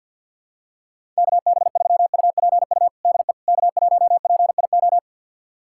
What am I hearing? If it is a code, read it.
Morse code, heard as a keyed tone, about 48 wpm, 700 Hz. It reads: KB3UPU DE K1PIO
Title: USA Callsign DE Callsign 50wpm